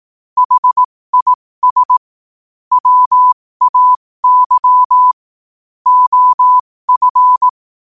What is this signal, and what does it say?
Morse code, heard as a keyed tone, about 18 wpm, spaced as for 15 wpm, 1000 Hz. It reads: HIS WAY OF
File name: qotdfiles / 15wpm